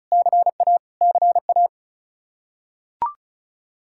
Morse code, heard as a keyed tone, about 34 wpm, 700 Hz. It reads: CA CA
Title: Practice QSO - Exchange 1 - Beginning - 35wpm